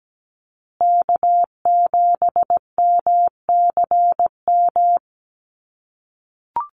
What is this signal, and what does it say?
Morse code, heard as a keyed tone, about 17 wpm, 700 Hz. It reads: K7MCM
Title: USA Call Signs Repeated 3x 17wpm